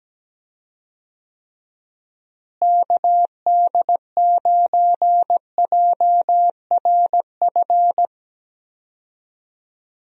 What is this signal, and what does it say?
Morse code, heard as a keyed tone, about 17 wpm, 700 Hz. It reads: KD9JRF
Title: USA Call Signs - 6 Characters - Repeated 3x - 17wpm